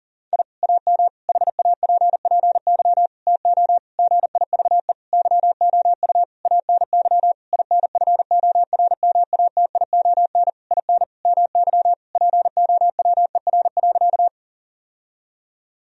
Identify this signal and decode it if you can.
Morse code, heard as a keyed tone, about 37 wpm, 700 Hz. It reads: I AM HAPPY TO GIVE YOU ANY INFORMATION IN MY POWER.
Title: Sentences from Top 1000 Words - Review Part 3 - 40wpm